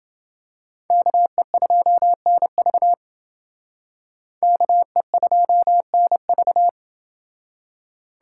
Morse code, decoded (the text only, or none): KE2NV KE2NV